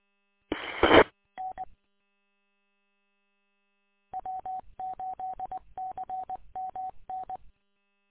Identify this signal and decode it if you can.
Morse code, heard as a keyed tone, about 20 wpm, 750 Hz. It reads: IN W8CMN
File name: w8cmn- -combined